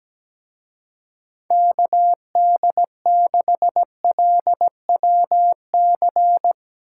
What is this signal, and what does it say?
Morse code, heard as a keyed tone, about 17 wpm, 700 Hz. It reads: KD6LWC
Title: Speed Racing - USA 2x3 Call Signs 17wpm